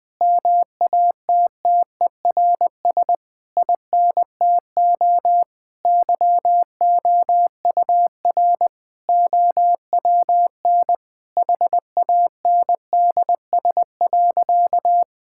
Morse code, heard as a keyed tone, about 20 wpm, 700 Hz. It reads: MATTERS INTO YOUR OWN HANDS.
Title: Sentences with 7-Word Phrases - 20wpm